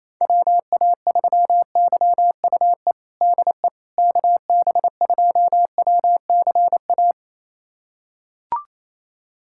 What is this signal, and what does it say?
Morse code, heard as a keyed tone, about 28 wpm, 700 Hz. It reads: WA3YUE DE KB2WCA
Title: USA Callsign DE Callsign 28wpm